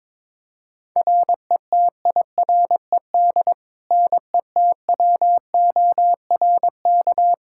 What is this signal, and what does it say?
Morse code, heard as a keyed tone, about 22 wpm, 700 Hz. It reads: RETIRED NETWORK
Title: Practice QSO - Exchange 3 - More Info - 22wpm